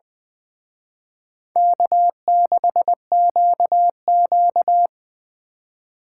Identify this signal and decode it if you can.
Morse code, heard as a keyed tone, about 23 wpm, 690 Hz. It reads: K6QQ
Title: Speed Racing - USA 1x2 Call Signs 20wpm